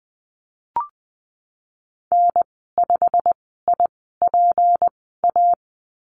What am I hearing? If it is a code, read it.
Morse code, heard as a keyed tone, about 20 wpm, spaced as for 15 wpm, 700 Hz. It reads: N5IPA
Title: 161 Lesson 26g - Letter g - Call Signs 20wpm